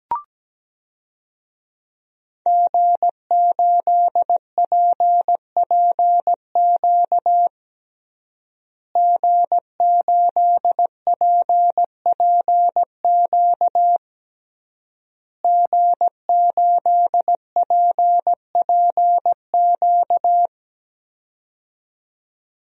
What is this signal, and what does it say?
Morse code, heard as a keyed tone, about 17 wpm, 700 Hz. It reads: G8PPQ G8PPQ G8PPQ
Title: DXCC-100 Call Signs - 5 Characters - Repeated 3x - 17wpm